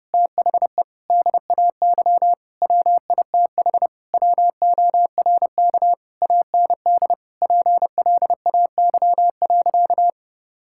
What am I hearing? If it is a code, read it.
Morse code, heard as a keyed tone, about 30 wpm, 700 Hz. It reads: THE DAY WITH WORK AND PLAY.
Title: Sentences from Top 1000 Words - Review Part 6 - 30wpm